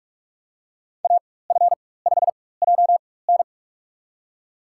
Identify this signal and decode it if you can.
Morse code, heard as a keyed tone, about 46 wpm, spaced as for 21 wpm, 700 Hz. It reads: AF5JN
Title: USA Call Signs - 5 Characters - 4x Character Spacing - 45wpm